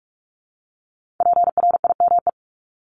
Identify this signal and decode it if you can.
Morse code, heard as a keyed tone, about 46 wpm, 700 Hz. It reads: PRIME
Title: Multi-letter Phonemes 45wpm